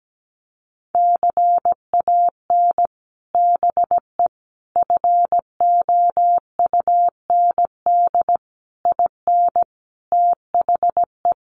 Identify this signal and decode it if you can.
Morse code, heard as a keyed tone, about 17 wpm, 700 Hz. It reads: CAN BE FOUND IN THE
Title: Sets of 5 Words Encore 17wpm